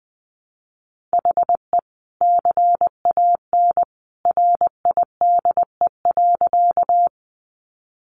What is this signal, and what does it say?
Morse code, heard as a keyed tone, about 20 wpm, 700 Hz. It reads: HE CAN RIDE.
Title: Sentences from Top 1200 Words 20wpm